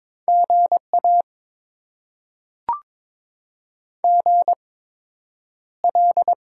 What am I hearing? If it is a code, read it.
Morse code, heard as a keyed tone, about 22 wpm, spaced as for 9 wpm, 700 Hz. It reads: GA GL
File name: Top 90 QSO Elements - 8x Character Spacing - 22wpm